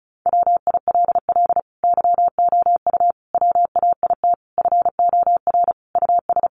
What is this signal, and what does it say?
Morse code, heard as a keyed tone, about 34 wpm, 700 Hz. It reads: WILL YOU WAIT FOR US
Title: Sentences from Top 900 Words 35wpm